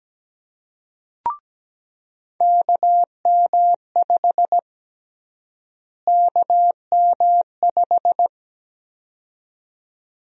Morse code, heard as a keyed tone, about 17 wpm, 700 Hz. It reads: KM5 KM5
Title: USA Call Sign Prefixes - Repeated 2x - 17wpm